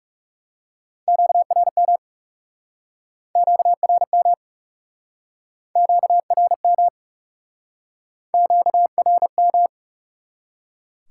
Morse code, heard as a keyed tone, about 40 wpm, 700 Hz. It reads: QRM QRM QRM QRM